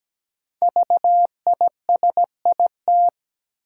Morse code, heard as a keyed tone, about 17 wpm, 700 Hz. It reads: VISIT
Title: Sentences from Top 900 Words - New Word Drill - 17wpm